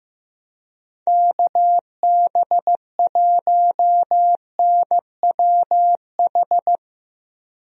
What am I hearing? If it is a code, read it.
Morse code, heard as a keyed tone, about 15 wpm, 700 Hz. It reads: KB1NWH
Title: USA Call Signs - 6 Characters - 15wpm